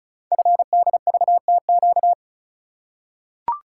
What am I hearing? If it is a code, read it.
Morse code, heard as a keyed tone, about 36 wpm, 690 Hz. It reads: FDVTQ
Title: Code Groups - 5 Letters - 35wpm